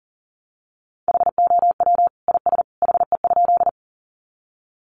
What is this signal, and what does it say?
Morse code, heard as a keyed tone, about 40 wpm, 700 Hz. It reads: HOW IS HE?
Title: Sentences from Top 100 Words 40wpm